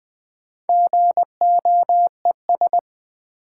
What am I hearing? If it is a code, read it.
Morse code, heard as a keyed tone, about 20 wpm, 700 Hz. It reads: GOES
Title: Sentences from Top 1000 Words - Review Part 5 - New Word Drill - 20wpm